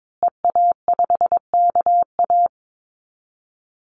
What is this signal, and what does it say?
Morse code, heard as a keyed tone, about 22 wpm, 700 Hz. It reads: EA5KA
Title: Call Signs Repeated 2x - 22wpm